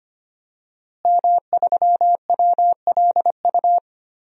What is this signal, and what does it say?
Morse code, heard as a keyed tone, about 25 wpm, 700 Hz. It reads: M3WLU
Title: DXCC-100 Call Signs - 5 Characters - Repeated 3x - 25wpm